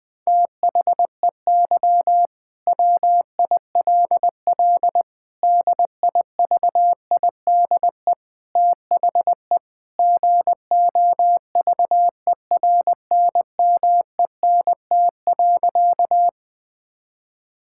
Morse code, heard as a keyed tone, about 20 wpm, 700 Hz. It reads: THEY WILL DIVIDE THE GOVERNMENT.
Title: Sentences from Top 1600 Words 20wpm